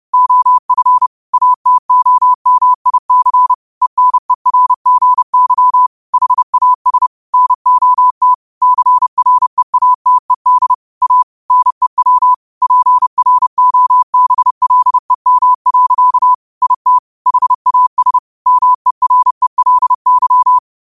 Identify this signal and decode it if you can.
Morse code, heard as a keyed tone, about 30 wpm, 1000 Hz. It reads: OF ATOMIC ENERGY HAS NOT CREATED A NEW PROBLEM. IT HAS MERELY
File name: 30wpm